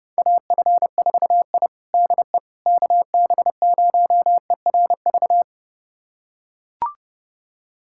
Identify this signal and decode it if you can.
Morse code, heard as a keyed tone, about 30 wpm, 700 Hz. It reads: AF4I DE KB0ERV
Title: USA Callsign DE Callsign 30wpm